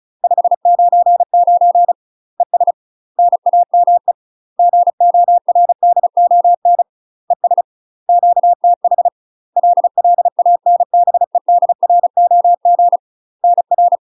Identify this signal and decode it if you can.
Morse code, heard as a keyed tone, about 35 wpm, 700 Hz. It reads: 599 ES NAME GORDON ES QTH LLANBEDROG NR
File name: UK MASTER QSOs TRAINING / 35WPM / UK MASTER QSOs 0001